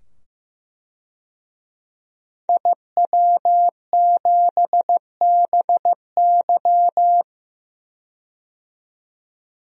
Morse code, heard as a keyed tone, about 15 wpm, 700 Hz. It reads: IW7BY
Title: DXCC-100 Call Signs - 5 Characters - 15wpm